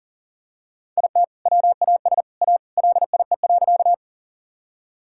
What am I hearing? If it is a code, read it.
Morse code, heard as a keyed tone, about 40 wpm, 700 Hz. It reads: IT WAS A LIE.
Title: Sentences from Top 1600 Words 40wpm